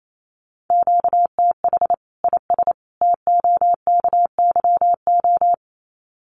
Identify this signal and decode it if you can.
Morse code, heard as a keyed tone, about 28 wpm, 700 Hz. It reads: QTH IS TOKYO
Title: Common QSO Phrases 28wpm